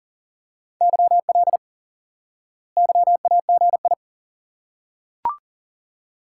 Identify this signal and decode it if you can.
Morse code, heard as a keyed tone, about 37 wpm, 700 Hz. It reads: YL YAGI
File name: IQE Course 3 - Lesson 89 - YAGI - Two Elements - 40wpm